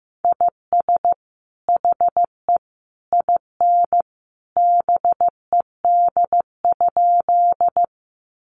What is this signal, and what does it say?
Morse code, heard as a keyed tone, about 15 wpm, 700 Hz. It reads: IS HE IN BED?
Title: Sentences from Top 500 Words 15wpm